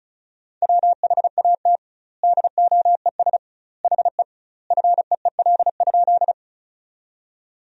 Morse code, heard as a keyed tone, about 36 wpm, 700 Hz. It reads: WHAT DOES HE FEEL?
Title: 2x Word-Spacing - Sentences from Top 400 Words - 35wpm